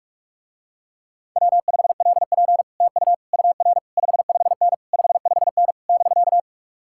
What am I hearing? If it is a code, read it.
Morse code, heard as a keyed tone, about 49 wpm, 700 Hz. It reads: W5LP TU UR 55N 55N <BK>